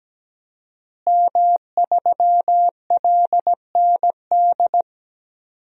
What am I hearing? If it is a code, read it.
Morse code, heard as a keyed tone, about 17 wpm, 700 Hz. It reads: M3LND
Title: DXCC-100 Call Signs - 5 Characters - Repeated 3x - 17wpm